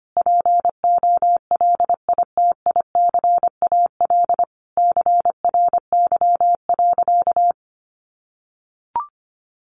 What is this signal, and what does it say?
Morse code, heard as a keyed tone, about 25 wpm, 700 Hz. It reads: POLITICAL CRY.
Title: Sentences from Top 1400 Words 25wpm